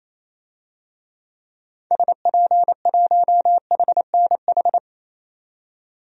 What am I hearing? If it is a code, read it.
Morse code, heard as a keyed tone, about 28 wpm, 700 Hz. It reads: SP1HNH
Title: DXCC-100 Call Signs - 6 Characters - 28wpm